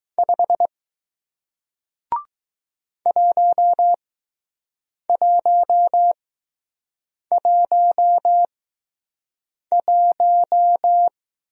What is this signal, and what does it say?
Morse code, heard as a keyed tone, about 20 wpm, spaced as for 6 wpm, 700 Hz. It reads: 5 1111